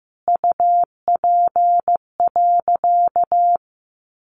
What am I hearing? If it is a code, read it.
Morse code, heard as a keyed tone, about 15 wpm, 700 Hz. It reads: UP.
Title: Sentences from Top 1000 Words 15wpm